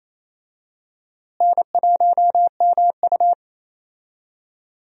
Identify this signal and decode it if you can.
Morse code, heard as a keyed tone, about 28 wpm, 700 Hz. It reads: N1MU
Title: USA Call Signs - 4 Characters - Repeated 2x - 28wpm